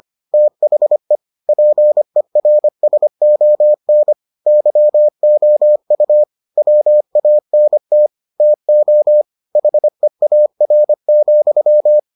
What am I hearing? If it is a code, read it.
Morse code, heard as a keyed tone, about 25 wpm, 600 Hz. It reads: THE PERSON YOU WANT TO HEAR,